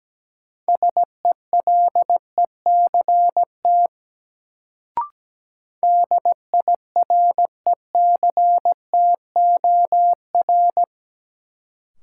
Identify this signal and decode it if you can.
Morse code, heard as a keyed tone, about 17 wpm, 700 Hz. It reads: SELECT DIRECTOR
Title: Sentences from Top 1300 Words - New Word Drill Alt - 17wpm